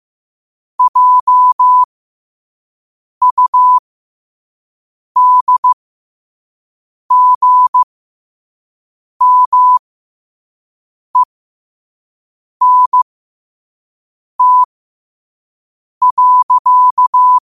Morse code, heard as a keyed tone, about 15 wpm, spaced as for 5 wpm, 1000 Hz. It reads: JUDGMENT.